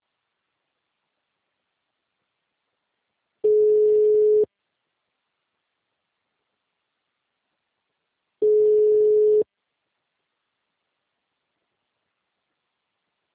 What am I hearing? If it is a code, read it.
Ringback tone (Europe, 425 Hz).